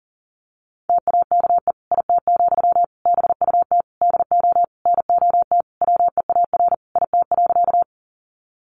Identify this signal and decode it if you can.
Morse code, heard as a keyed tone, about 40 wpm, 700 Hz. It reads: TAKE IT, BUT DO NOT WEAR IT.
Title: Sentences from Top 1500 Words 40wpm